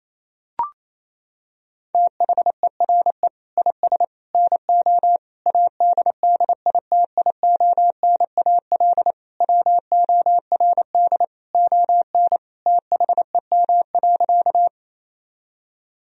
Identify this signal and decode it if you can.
Morse code, heard as a keyed tone, about 28 wpm, 700 Hz. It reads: THERE IS NO ADDITIONAL WORD ON THEM.
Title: Sentences from Top 1000 Words - Review Part 4 - 28wpm